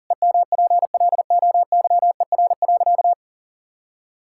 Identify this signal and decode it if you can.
Morse code, heard as a keyed tone, about 40 wpm, 700 Hz. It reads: EMPLOYER.
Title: Sentences from Top 1600 Words 40wpm